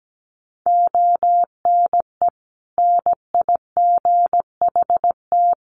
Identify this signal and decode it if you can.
Morse code, heard as a keyed tone, about 17 wpm, 700 Hz. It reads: ONE NIGHT
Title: Sentences from Top 300 Words 17wpm